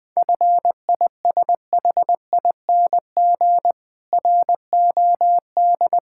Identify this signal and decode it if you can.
Morse code, heard as a keyed tone, about 20 wpm, 690 Hz. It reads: FISHING ROD
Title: Difficult Words with Hints - 20wpm